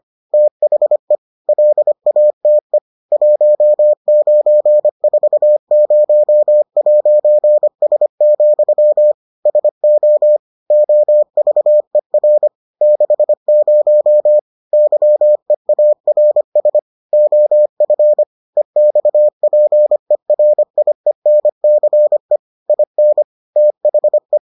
Morse code, heard as a keyed tone, about 25 wpm, 600 Hz. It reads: THE LATE 1940'S, SO OVER 60 YEARS OF EXPERIENCE IN THE